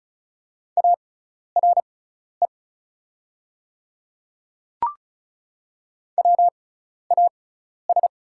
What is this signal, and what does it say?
Morse code, heard as a keyed tone, about 36 wpm, spaced as for 12 wpm, 700 Hz. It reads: ARE WAS
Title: 6x Character Spacing - Top 100 Words - 35wpm